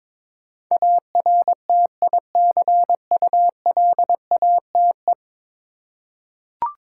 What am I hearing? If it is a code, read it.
Morse code, heard as a keyed tone, about 22 wpm, 700 Hz. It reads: ARTICULATE
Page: morse-code-ninja-practice-22wpm